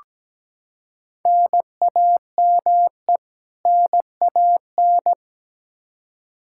Morse code, heard as a keyed tone, about 17 wpm, 700 Hz. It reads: NAME NAN
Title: Practice QSO - Exchange 1 - Beginning - 17wpm